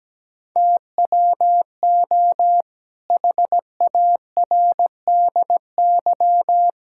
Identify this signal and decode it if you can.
Morse code, heard as a keyed tone, about 17 wpm, 700 Hz. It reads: TWO HARDY
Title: Pangram Challenge v2 - 17wpm